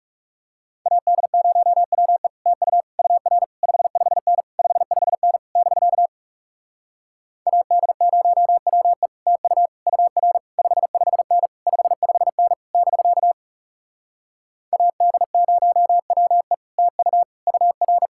Morse code, heard as a keyed tone, about 37 wpm, 700 Hz. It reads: AD0WE TU UR 55N 55N <BK> AD0WE TU UR 55N 55N <BK> AD0WE TU UR
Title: Speed-Racing Common QSO Phrases - 30wpm